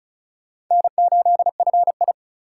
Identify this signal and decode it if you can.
Morse code, heard as a keyed tone, about 42 wpm, 700 Hz. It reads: N8FI
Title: Speed Racing - USA 1x2 Call Signs 35wpm